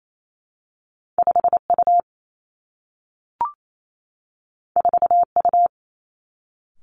Morse code, heard as a keyed tone, about 28 wpm, 700 Hz. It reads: 5 U 4 U
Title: Two Challenging Letter-Number Combinations 28wpm